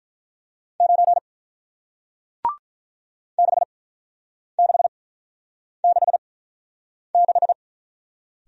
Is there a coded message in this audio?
Yes: Morse code from 0.8 to 7.5 s, heard as a keyed tone, about 47 wpm, spaced as for 9 wpm, 700 Hz. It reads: 9 6666